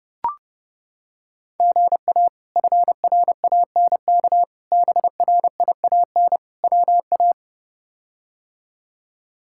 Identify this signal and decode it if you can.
Morse code, heard as a keyed tone, about 30 wpm, 700 Hz. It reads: GA FRANK BRIAN WA
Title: K1USN SST QSO Practice - 30wpm